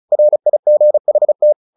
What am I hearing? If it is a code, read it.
Morse code, heard as a keyed tone, about 34 wpm, 600 Hz. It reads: RIGHT
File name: the old man and the sea 2 35wpm